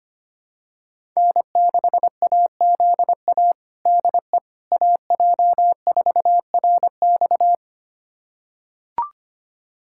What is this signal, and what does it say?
Morse code, heard as a keyed tone, about 25 wpm, 700 Hz. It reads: N6AZA DE AJ4RX
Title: USA Callsign DE Callsign - 5 Characters - 25wpm